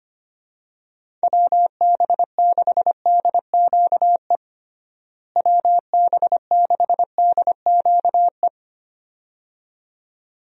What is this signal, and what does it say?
Morse code, heard as a keyed tone, about 25 wpm, 700 Hz. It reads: WB6DQE WB6DQE
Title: USA Call Signs - 6 Characters - Repeated 2x - 25wpm